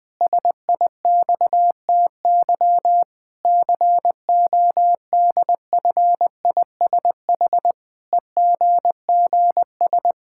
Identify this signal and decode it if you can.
Morse code, heard as a keyed tone, about 20 wpm, 700 Hz. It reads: SIXTY CODFISH EGGS
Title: Pangram Challenge v2 - 20wpm